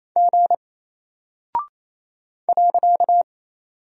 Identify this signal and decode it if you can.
Morse code, heard as a keyed tone, about 28 wpm, 700 Hz. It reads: G .
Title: Single Character (Letter:Number:Punctuation:Prosign) Alt 28wpm